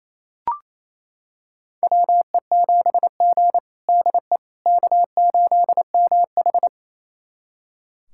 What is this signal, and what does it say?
Morse code, heard as a keyed tone, about 28 wpm, 700 Hz. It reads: WE7G DE K8MH
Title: USA Callsign DE Callsign - 4 Characters - 28wpm